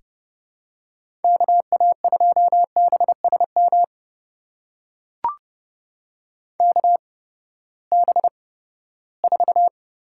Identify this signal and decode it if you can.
Morse code, heard as a keyed tone, about 30 wpm, spaced as for 17 wpm, 700 Hz. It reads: KA2BSM KB4
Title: USA Call Signs - 6 Characters - 8x Character Spacing - 30wpm